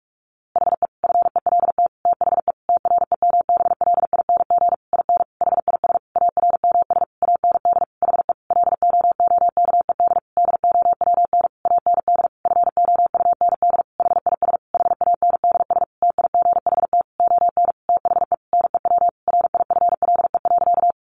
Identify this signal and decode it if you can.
Morse code, heard as a keyed tone, about 44 wpm, 700 Hz. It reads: HE FELT THE TREMBLING IN HIS ARMS AND HE LOOKED DOWN AND FOUND HIS HANDS TIGHT ON THE NEW RIFLE.
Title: A Sound Of Thunder 45wpm